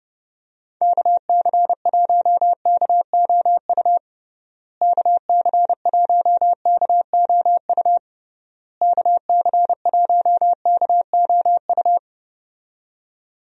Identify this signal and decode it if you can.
Morse code, heard as a keyed tone, about 30 wpm, 700 Hz. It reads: KC1KOU KC1KOU KC1KOU
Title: USA Call Signs - 6 Characters - Repeated 3x - 30wpm